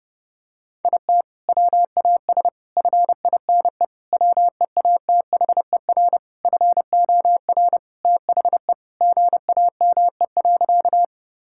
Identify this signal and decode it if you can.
Morse code, heard as a keyed tone, about 30 wpm, 700 Hz. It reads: IT WAS FINE WEATHER FOR THE GAME.
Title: Sentences from Top 1000 Words - Review Part 5 - 30wpm